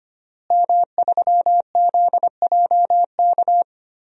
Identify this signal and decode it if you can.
Morse code, heard as a keyed tone, about 25 wpm, 700 Hz. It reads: M3ZJK
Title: DXCC-100 Call Signs - 5 Characters - Repeated 2x - 25wpm